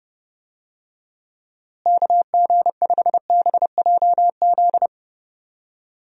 Morse code, heard as a keyed tone, about 31 wpm, 700 Hz. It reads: KG5BJZ
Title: USA Call Signs - 6 Characters - Repeated 2x - 30wpm